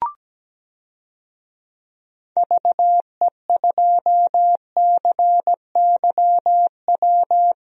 Morse code, heard as a keyed tone, about 17 wpm, 700 Hz. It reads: VE2CYW
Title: DXCC-100 Call Signs - 6 Characters - Repeated 3x - 17wpm